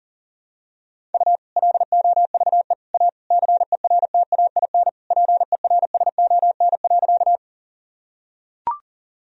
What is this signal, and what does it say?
Morse code, heard as a keyed tone, about 40 wpm, 700 Hz. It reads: U LOVE A CERTAIN PERSON.
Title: Sentences from Top 500 Words and Abbreviations 40wpm